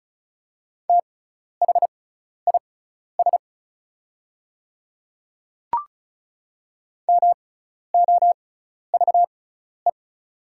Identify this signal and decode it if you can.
Morse code, heard as a keyed tone, about 36 wpm, spaced as for 12 wpm, 700 Hz. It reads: THIS MOVE
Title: 6x Character Spacing - Top 300 Words - 35wpm